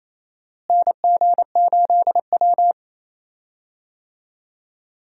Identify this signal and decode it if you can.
Morse code, heard as a keyed tone, about 28 wpm, 700 Hz. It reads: NG8W
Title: USA Call Signs - 4 Characters - Repeated 2x - 28wpm